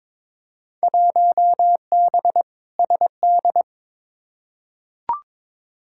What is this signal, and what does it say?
Morse code, heard as a keyed tone, about 22 wpm, 700 Hz. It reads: 1B SD
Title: Field Day Exchanges Encore - 22wpm